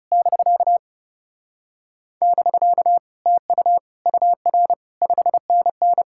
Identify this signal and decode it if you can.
Morse code, heard as a keyed tone, about 40 wpm, 700 Hz. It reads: <BK> <BK> TU UR 5NN